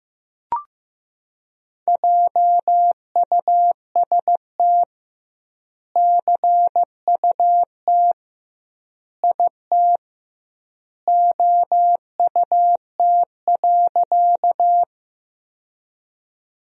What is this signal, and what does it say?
Morse code, heard as a keyed tone, about 15 wpm, 700 Hz. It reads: JUST CUT IT OUT.
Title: 2x Word-Spacing - Sentences from Top 300 Words - 15wpm